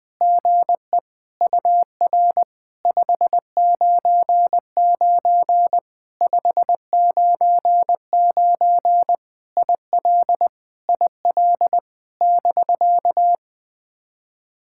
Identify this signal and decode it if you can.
Morse code, heard as a keyed tone, about 21 wpm, 700 Hz. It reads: GE UR 599 599 IL IL <BK>
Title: POTA QSO Practice - 20wpm